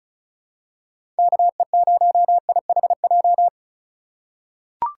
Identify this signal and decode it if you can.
Morse code, heard as a keyed tone, about 36 wpm, 700 Hz. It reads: KE0IHJ
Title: USA Call Signs - 6 Characters - 2x Character Spacing - 35wpm